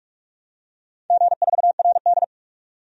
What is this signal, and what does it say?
Morse code, heard as a keyed tone, about 46 wpm, 700 Hz. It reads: G4RD
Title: DXCC-100 Call Signs - 4 Characters - 45wpm